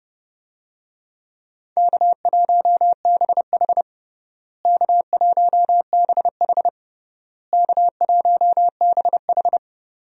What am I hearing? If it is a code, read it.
Morse code, heard as a keyed tone, about 30 wpm, 700 Hz. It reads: K1BH K1BH K1BH
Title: USA Call Signs - 4 Characters - Repeated 3x - 30wpm